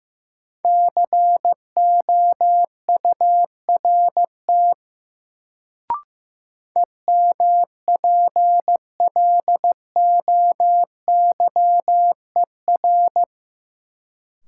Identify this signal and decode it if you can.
Morse code, heard as a keyed tone, about 15 wpm, 700 Hz. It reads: COURT EMPLOYER
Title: Sentences from Top 1600 Words - New Word Drill Alt - 15wpm